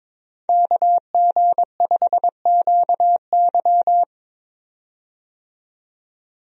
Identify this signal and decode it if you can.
Morse code, heard as a keyed tone, about 22 wpm, 700 Hz. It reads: KG5QY
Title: USA Call Signs - 5 Characters - 22wpm